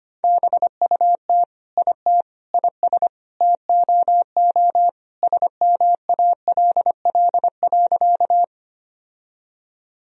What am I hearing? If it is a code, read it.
Morse code, heard as a keyed tone, about 25 wpm, 700 Hz. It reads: BUT IT IS TOO SMALL.
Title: Sentences from Top 1000 Words - Review Part 1 - 25wpm